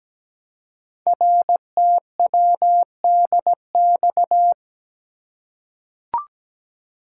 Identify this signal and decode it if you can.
Morse code, heard as a keyed tone, about 17 wpm, 700 Hz. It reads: RTWDX
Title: Code Groups - 5 Letters - 17wpm